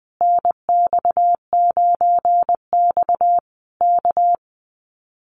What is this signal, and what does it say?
Morse code, heard as a keyed tone, about 20 wpm, 700 Hz. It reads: NX9X K
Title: Common QSO Phrases Encore 20wpm